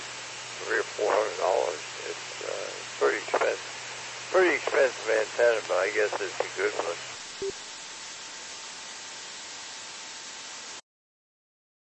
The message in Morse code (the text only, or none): none